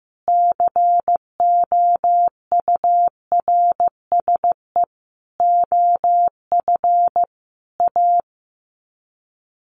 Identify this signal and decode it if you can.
Morse code, heard as a keyed tone, about 15 wpm, 700 Hz. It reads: COURSE OF A
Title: Sets of 5 Words Encore 15wpm